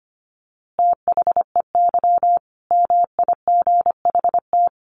Morse code, heard as a keyed tone, about 25 wpm, 700 Hz. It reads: THEY MIGHT
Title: A Sound Of Thunder 25wpm